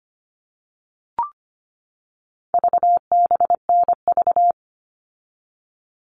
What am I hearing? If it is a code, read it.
Morse code, heard as a keyed tone, about 25 wpm, 700 Hz. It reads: VBNV
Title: All Cut Numbers - 4-Digit 25wpm